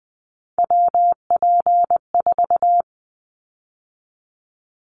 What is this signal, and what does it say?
Morse code, heard as a keyed tone, about 20 wpm, 700 Hz. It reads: WP4
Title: Callsign Prefixes for Top 25 DXCC - 20wpm